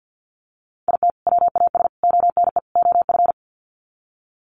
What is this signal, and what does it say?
Morse code, heard as a keyed tone, about 48 wpm, 700 Hz. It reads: IT WAS ONE OF
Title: Sets of 4 Words 50wpm